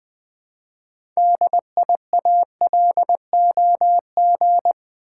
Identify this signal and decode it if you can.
Morse code, heard as a keyed tone, about 20 wpm, 700 Hz. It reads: DIALOG
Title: Build A Word - 3 Letters At A Time - 8-Letter Words - 20wpm